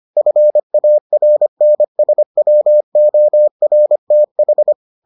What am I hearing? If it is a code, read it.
Morse code, heard as a keyed tone, about 25 wpm, 600 Hz. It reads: FARNSWORTH